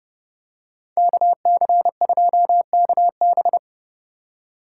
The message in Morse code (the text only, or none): KC2KB